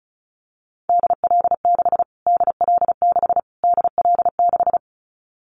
Morse code, heard as a keyed tone, about 34 wpm, 700 Hz. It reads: DL6 DL6 DL6
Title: Callsign Prefixes for Top 25 DXCC - Repeated 3x - 35wpm